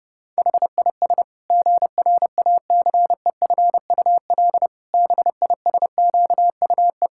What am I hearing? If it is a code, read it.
Morse code, heard as a keyed tone, about 32 wpm, 700 Hz. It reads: HIS GRACEFUL BISQUE
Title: Pangram Challenge v2 - 30wpm